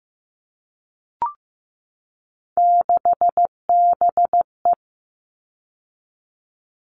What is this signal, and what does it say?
Morse code, heard as a keyed tone, about 15 wpm, 700 Hz. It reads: 6BE
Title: All Cut Numbers - 3-Digit 15wpm